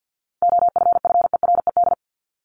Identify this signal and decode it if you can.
Morse code, heard as a keyed tone, about 50 wpm, 700 Hz. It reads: OFFERED
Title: ED Suffix - Top 100 Words with Base Words 50wpm